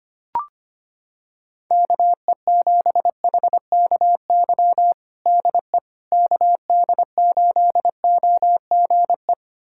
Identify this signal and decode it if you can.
Morse code, heard as a keyed tone, about 25 wpm, 700 Hz. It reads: KE7HKY DE KD8OGE
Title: USA Callsign DE Callsign - 6 Characters - 25wpm